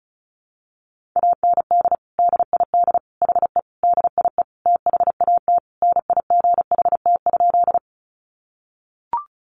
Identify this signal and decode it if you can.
Morse code, heard as a keyed tone, about 34 wpm, 700 Hz. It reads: AND DID HE DIE THAT NIGHT?
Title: Sentences from Top 1300 Words 35wpm